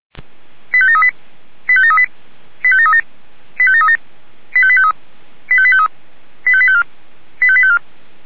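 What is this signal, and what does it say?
Five-tone selective call (ZVEI), heard as quick runs of five tones, tones of about 80 ms. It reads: 85738 85738 85737 85737 85853 85853 85854 85854
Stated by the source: ZVEI five-tone selective call 85726, received by SDR.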